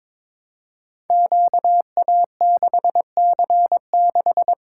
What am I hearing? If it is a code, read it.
Morse code, heard as a keyed tone, about 22 wpm, 700 Hz. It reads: QA6C6
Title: Code Groups - 5 Letter:Numbers - 22wpm